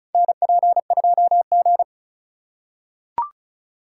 Morse code, heard as a keyed tone, about 36 wpm, 700 Hz. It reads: NP2G
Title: USA Call Signs - 4 Characters - 2x Character Spacing - 35wpm